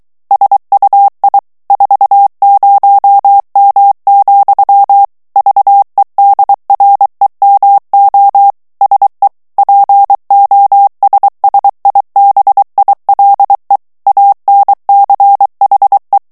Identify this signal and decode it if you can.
Morse code, heard as a keyed tone, about 23 wpm, 800 Hz. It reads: SUI 40M, VEDREMO SE POSSIBILE ANCHE
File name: morse mp3 / ITALIEN texte clair 1 vitesse 22 mots minute